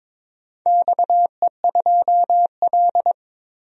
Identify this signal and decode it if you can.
Morse code, heard as a keyed tone, about 22 wpm, 700 Hz. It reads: XE2L
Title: DXCC-100 Call Signs - 4 Characters - 22wpm